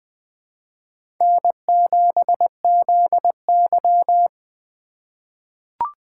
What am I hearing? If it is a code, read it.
Morse code, heard as a keyed tone, about 21 wpm, 700 Hz. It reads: N7ZY
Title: USA Call Signs - 4x Character Spacing - 20wpm